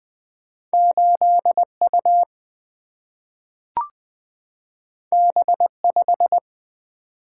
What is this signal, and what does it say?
Morse code, heard as a keyed tone, about 20 wpm, 700 Hz. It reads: 8 U B 5
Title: Single-Letter Single-Number 20wpm